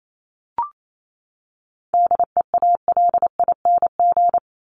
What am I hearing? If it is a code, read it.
Morse code, heard as a keyed tone, about 28 wpm, 700 Hz. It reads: DEALING
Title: ING Suffix - Top 100 Words with Base Words 28wpm